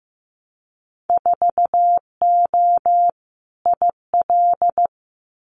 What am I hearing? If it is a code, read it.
Morse code, heard as a keyed tone, about 15 wpm, 700 Hz. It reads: 4O IL
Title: Winter Field Day Exchanges 15wpm